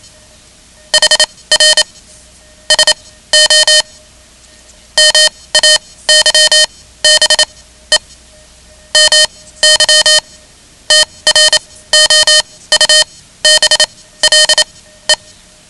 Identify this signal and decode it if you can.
Morse code, heard as a keyed tone, about 27 wpm, spaced as for 16 wpm, 3050 Hz. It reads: HR SO MAYBE MY TROUBLE
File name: ham radio / CodeQuick / Code Quick 10